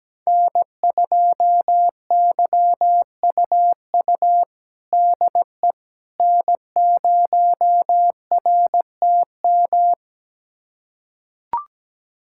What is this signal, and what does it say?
Morse code, heard as a keyed tone, about 17 wpm, 700 Hz. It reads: N2YUU DE N0RTM
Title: USA Callsign DE Callsign - 5 Characters - 17wpm